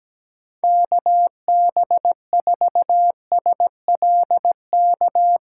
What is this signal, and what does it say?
Morse code, heard as a keyed tone, about 17 wpm, 700 Hz. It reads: KB4SLK
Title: USA Call Signs - 6 Characters - 2x Character Spacing - 17wpm